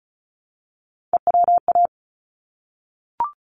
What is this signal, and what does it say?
Morse code, heard as a keyed tone, about 34 wpm, 700 Hz. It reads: EWA
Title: ARRL Sections 35wpm